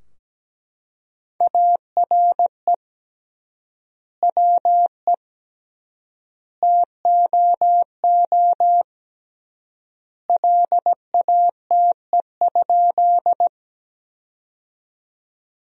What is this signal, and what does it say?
Morse code, heard as a keyed tone, about 17 wpm, 700 Hz. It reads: ARE WE TOO LATE?
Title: 3x Word-Spacing - Sentences from Top 300 Words - 17wpm